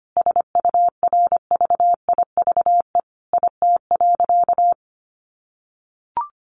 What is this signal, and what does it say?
Morse code, heard as a keyed tone, about 25 wpm, 700 Hz. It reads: SURVIVE IT.
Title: Sentences from Top 1300 Words 25wpm